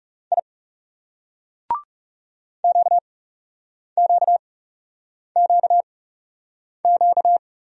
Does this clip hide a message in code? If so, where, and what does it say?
Yes: Morse code from 0.3 to 7.4 s, heard as a keyed tone, about 38 wpm, spaced as for 8 wpm, 700 Hz. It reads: I QQQQ